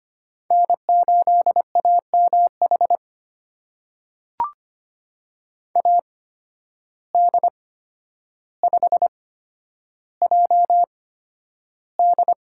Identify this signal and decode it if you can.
Morse code, heard as a keyed tone, about 25 wpm, spaced as for 11 wpm, 700 Hz. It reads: N8AMH AD5JD
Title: USA Call Signs - 5 Characters - 8x Character Spacing - 25wpm